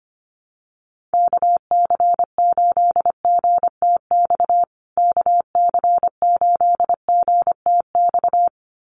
Morse code, heard as a keyed tone, about 25 wpm, 700 Hz. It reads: KC8GTX KC8GTX
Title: USA Call Signs Repeated 2x 25wpm